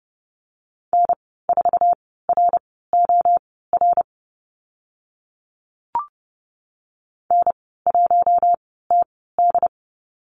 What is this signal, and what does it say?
Morse code, heard as a keyed tone, about 30 wpm, spaced as for 17 wpm, 700 Hz. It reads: N4ROR N1TD
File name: Lesson 14g - Number 2 - Call Signs 30wpm